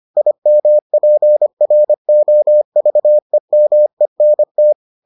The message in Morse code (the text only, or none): IMPROVEMENT